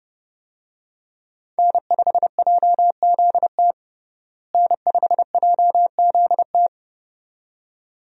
Morse code, heard as a keyed tone, about 29 wpm, 700 Hz. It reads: N5JZT N5JZT
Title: USA Call Signs - 5 Characters - Repeated 2x - 30wpm